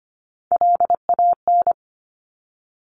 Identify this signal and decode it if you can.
Morse code, heard as a keyed tone, about 25 wpm, 700 Hz. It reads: LAN
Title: Top Three Letter Combinations in Words 25wpm